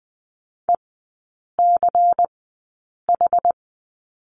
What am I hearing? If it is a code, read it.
Morse code, heard as a keyed tone, about 20 wpm, spaced as for 8 wpm, 700 Hz. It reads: ECH
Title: 114 Lesson 19e - Letter w - Three Characters 20wpm